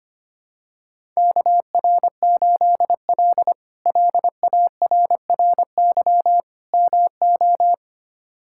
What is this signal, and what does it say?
Morse code, heard as a keyed tone, about 25 wpm, 700 Hz. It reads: KR8L LARRY MO
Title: K1USN SST QSO Practice - 25wpm